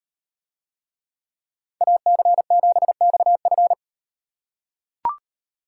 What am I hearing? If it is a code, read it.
Morse code, heard as a keyed tone, about 37 wpm, 700 Hz. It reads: AC7XF